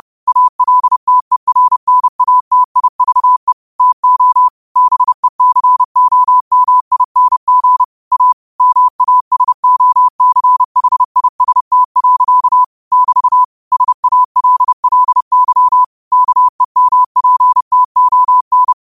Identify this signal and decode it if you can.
Morse code, heard as a keyed tone, about 30 wpm, 1000 Hz. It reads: ALTERNATIVE TO BECOMING A MASOCHIST. <BT> SALLY KEMPTON
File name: qotdfiles / 30wpm